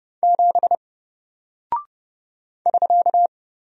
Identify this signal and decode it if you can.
Morse code, heard as a keyed tone, about 30 wpm, 700 Hz. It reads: 7 <SK>
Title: Single Character (Letter:Number:Punctuation:Prosign) Alt 30wpm